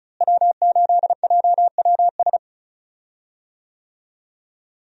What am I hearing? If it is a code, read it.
Morse code, heard as a keyed tone, about 34 wpm, 700 Hz. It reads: W8JWS
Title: USA Call Signs - 5 Characters - Repeated 2x - 35wpm